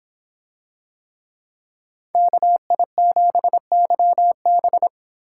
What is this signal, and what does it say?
Morse code, heard as a keyed tone, about 26 wpm, 700 Hz. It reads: KI7YB